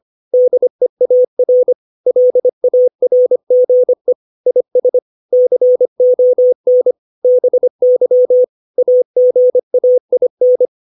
Morse code, heard as a keyed tone, about 25 wpm, 500 Hz. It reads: DEAR LARGE IS CON BY AGAIN
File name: 632 words / 632 Words 25WPM